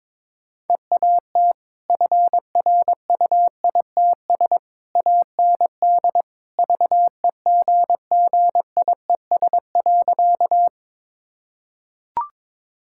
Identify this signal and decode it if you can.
Morse code, heard as a keyed tone, about 21 wpm, 700 Hz. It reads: EAT FRUITS AND VEGGIES.
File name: practice / 22wpm / Easy Sentences with Hints - 22wpm